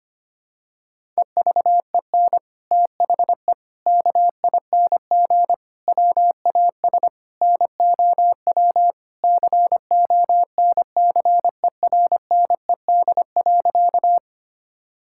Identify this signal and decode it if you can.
Morse code, heard as a keyed tone, about 25 wpm, 700 Hz. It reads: EVEN THE KING WAS NOW CONCERNED.
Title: Sentences from Top 1000 Words - Review Part 6 - 25wpm